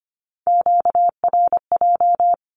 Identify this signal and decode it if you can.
Morse code, heard as a keyed tone, about 25 wpm, 700 Hz. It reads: QRJ
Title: Q-Codes with Spoken Meaning - 25wpm